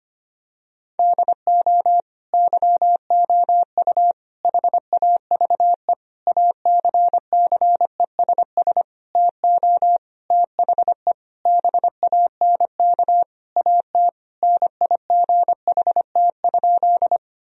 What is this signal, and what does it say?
Morse code, heard as a keyed tone, about 25 wpm, 700 Hz. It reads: DO YOU HAVE ACCESS TO THE BANK AT NIGHT?
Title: Sentences from Top 1000 Words - Review Part 4 - 25wpm